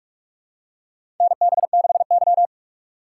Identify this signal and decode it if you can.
Morse code, heard as a keyed tone, about 46 wpm, 700 Hz. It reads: NB6Y
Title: USA Call Signs - 6x Character Spacing - 45wpm